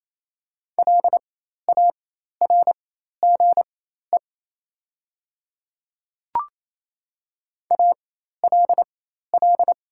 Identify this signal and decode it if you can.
Morse code, heard as a keyed tone, about 28 wpm, spaced as for 13 wpm, 700 Hz. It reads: LARGE ALL
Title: 4x Character Spacing - Top 200 Words - 28wpm